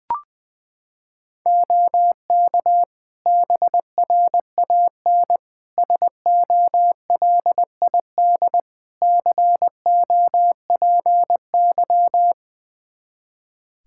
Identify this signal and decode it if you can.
Morse code, heard as a keyed tone, about 21 wpm, 700 Hz. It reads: OK BRAN SOLID COPY
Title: Practice QSO - Exchange 3 - More Info - 20wpm